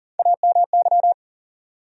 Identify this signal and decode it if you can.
Morse code, heard as a keyed tone, about 39 wpm, 700 Hz. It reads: AMY
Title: USA Common Names - Max 4 Letters - 40wpm